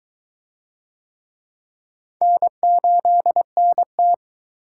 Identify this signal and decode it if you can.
Morse code, heard as a keyed tone, about 22 wpm, 700 Hz. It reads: N8NT